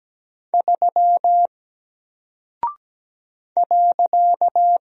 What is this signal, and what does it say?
Morse code, heard as a keyed tone, about 17 wpm, 700 Hz. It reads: 3 .
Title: Single Character (Letter:Number:Punctuation:Prosign) 17wpm